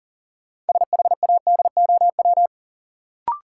Morse code, heard as a keyed tone, about 40 wpm, 700 Hz. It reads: SHADOW
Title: Sets of 5 Words Encore - New Word Drill - 40wpm